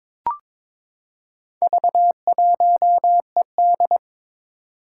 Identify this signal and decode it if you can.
Morse code, heard as a keyed tone, about 22 wpm, 700 Hz. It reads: V1ED
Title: Four Challenging Letter-Number Combinations 22wpm